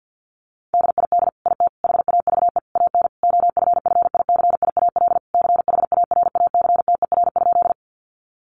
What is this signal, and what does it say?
Morse code, heard as a keyed tone, about 50 wpm, 700 Hz. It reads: DID IT HAVE AN OFFICIAL CHARACTER?
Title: Sentences from Top 1200 Words 50wpm